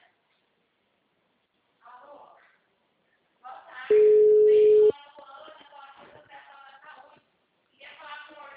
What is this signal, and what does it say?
Ringback tone (Europe, 425 Hz), with speech.
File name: out- -2022